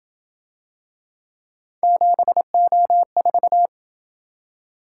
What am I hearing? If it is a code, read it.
Morse code, heard as a keyed tone, about 27 wpm, 700 Hz. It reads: 7O4